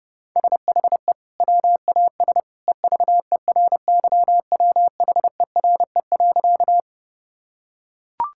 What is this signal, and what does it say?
Morse code, heard as a keyed tone, about 32 wpm, 700 Hz. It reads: SHE WAS EVERYWHERE.
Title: Sentences from Top 1600 Words 30wpm